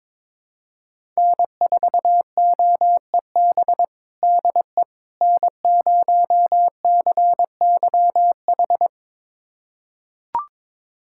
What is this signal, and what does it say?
Morse code, heard as a keyed tone, about 22 wpm, 700 Hz. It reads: N4OEB DE N0CYH
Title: USA Callsign DE Callsign 22wpm